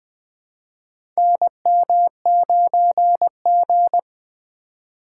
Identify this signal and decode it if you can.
Morse code, heard as a keyed tone, about 21 wpm, 700 Hz. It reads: NM9G
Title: USA Call Signs - 4 Characters - 3x Character Spacing - 20wpm